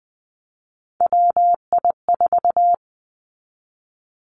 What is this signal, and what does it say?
Morse code, heard as a keyed tone, about 20 wpm, 700 Hz. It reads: WI4
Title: Three Challenging Letter-Number Combinations 20wpm